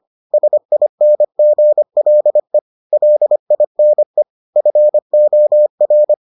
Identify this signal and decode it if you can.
Morse code, heard as a keyed tone, about 25 wpm, 600 Hz. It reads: SINGLE LINE FOR